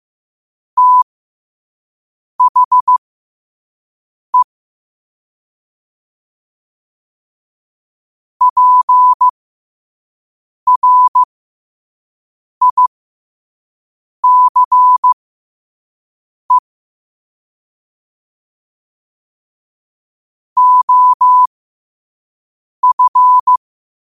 Morse code, heard as a keyed tone, about 15 wpm, spaced as for 5 wpm, 1000 Hz. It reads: THE PRICE OF